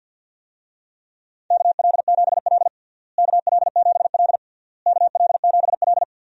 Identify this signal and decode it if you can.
Morse code, heard as a keyed tone, about 48 wpm, 700 Hz. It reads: KL7L KL7L KL7L
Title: DXCC-100 Call Signs - 4 Characters - Repeated 3x - 50wpm